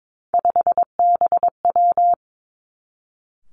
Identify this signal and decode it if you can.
Morse code, heard as a keyed tone, about 22 wpm, 700 Hz. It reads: 5BW
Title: Three Challenging Letter-Number Combinations 22wpm